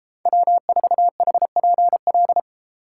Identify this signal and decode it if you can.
Morse code, heard as a keyed tone, about 32 wpm, 700 Hz. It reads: W4HPL